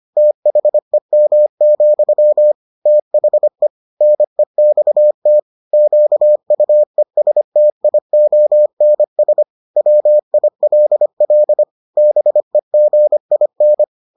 Morse code, heard as a keyed tone, about 25 wpm, 600 Hz. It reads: THEM, THE NEXT QUESTIONS WILL BEGIN